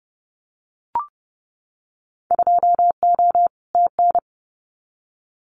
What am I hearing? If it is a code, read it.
Morse code, heard as a keyed tone, about 30 wpm, 700 Hz. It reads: 2O TN
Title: Winter Field Day Exchanges 30wpm